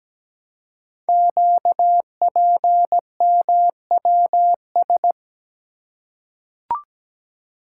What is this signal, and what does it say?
Morse code, heard as a keyed tone, about 17 wpm, 700 Hz. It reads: QPMWS
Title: Code Groups - 5 Letters - 17wpm